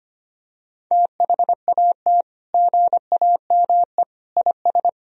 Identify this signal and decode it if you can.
Morse code, heard as a keyed tone, about 25 wpm, 700 Hz. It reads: THAT GAME IS
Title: Sentences from Top 1000 Words - Review Part 3 - 25wpm